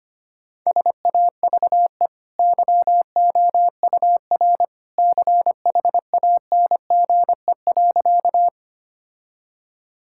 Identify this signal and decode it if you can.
Morse code, heard as a keyed tone, about 25 wpm, 700 Hz. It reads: SAVE YOUR CHANGE.
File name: Easy Sentences with Hints - 25wpm